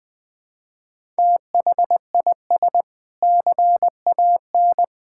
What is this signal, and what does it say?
Morse code, heard as a keyed tone, about 21 wpm, 700 Hz. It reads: THIS CAN
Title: Sentences with 3-Word Phrases - 20wpm